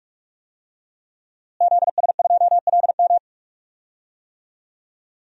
Morse code, heard as a keyed tone, about 41 wpm, 700 Hz. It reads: ZS2LM
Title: DXCC-100 Call Signs - 5 Characters - Repeated 3x - 45wpm